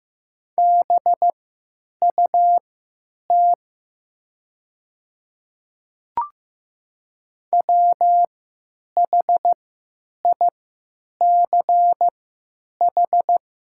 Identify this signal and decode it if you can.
Morse code, heard as a keyed tone, about 15 wpm, spaced as for 9 wpm, 700 Hz. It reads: BUT WHICH
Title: 3x Character Spacing - Top 100 Words - 15wpm